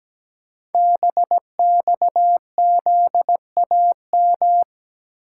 Morse code, heard as a keyed tone, about 17 wpm, 700 Hz. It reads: BXZAM
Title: Code Groups - 5 Letters - 17wpm